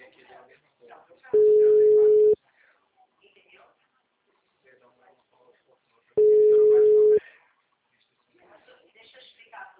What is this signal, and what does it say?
Ringback tone (Europe, 425 Hz), with speech.
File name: out- -2040- - - .73796